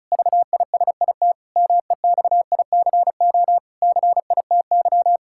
Morse code, heard as a keyed tone, about 34 wpm, 700 Hz. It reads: VISIT MEXICO CITY
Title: Pangram Challenge v2 - 35wpm